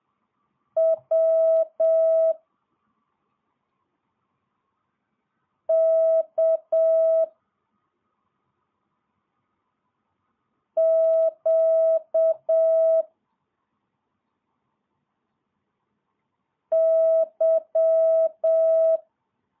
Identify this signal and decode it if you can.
Morse code, heard as a keyed tone, about 7 wpm, 650 Hz. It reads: W K Q Y
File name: jh8cbh / 2022cw